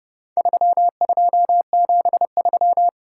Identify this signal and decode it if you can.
Morse code, heard as a keyed tone, about 29 wpm, 700 Hz. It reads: 3273